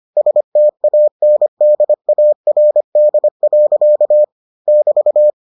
Morse code, heard as a keyed tone, about 25 wpm, 600 Hz. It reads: STANDARD. <BT>